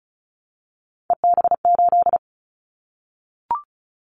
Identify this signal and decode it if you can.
Morse code, heard as a keyed tone, about 36 wpm, 700 Hz. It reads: EB8
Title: Callsign Prefixes for Top 50 DXCC - 35wpm